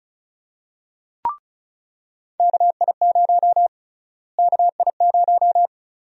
Morse code, heard as a keyed tone, about 36 wpm, 700 Hz. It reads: KI0 KI0
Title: USA Call Sign Prefixes - Repeated 2x - 35wpm